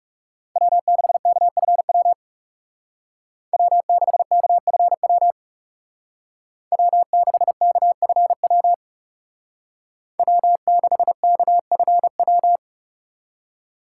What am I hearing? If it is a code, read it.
Morse code, heard as a keyed tone, about 34 wpm, 700 Hz. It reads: W6KFW W6KFW W6KFW W6KFW
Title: Speed Racing - USA 1x3 Call Signs 30wpm